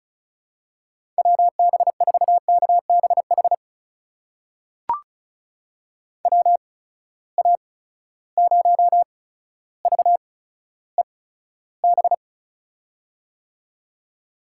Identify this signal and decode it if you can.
Morse code, heard as a keyed tone, about 36 wpm, spaced as for 15 wpm, 700 Hz. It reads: WB4KBH WA0VEB
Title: USA Call Signs - 6 Characters - 8x Character Spacing - 35wpm